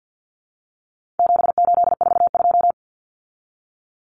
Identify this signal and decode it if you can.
Morse code, heard as a keyed tone, about 50 wpm, 700 Hz. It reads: 7842
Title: Numbers - 4-Digit 50wpm